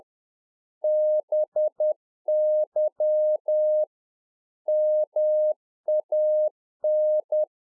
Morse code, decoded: BY MAN

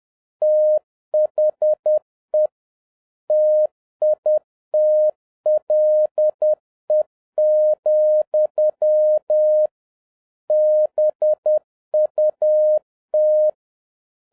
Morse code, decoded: THE TITLE, BUT

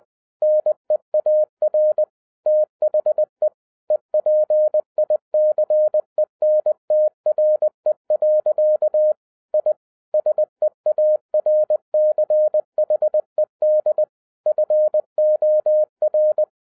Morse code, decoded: NEAR THE EPICENTRE. I SEARCHED FOR

